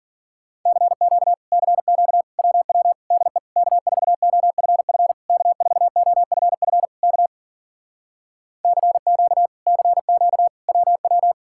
Morse code, heard as a keyed tone, about 52 wpm, 700 Hz. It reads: CQ CQ WW DE K4OFF K4OFF K CQ CQ WW